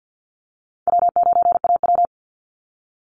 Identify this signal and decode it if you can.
Morse code, heard as a keyed tone, about 50 wpm, 700 Hz. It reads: W9AW
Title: Call Signs - 50wpm